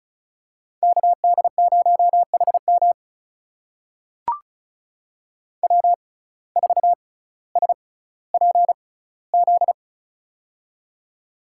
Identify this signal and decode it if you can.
Morse code, heard as a keyed tone, about 36 wpm, spaced as for 18 wpm, 700 Hz. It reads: KD0HM W4SPZ
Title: USA Call Signs - 5 Characters - 6x Character Spacing - 35wpm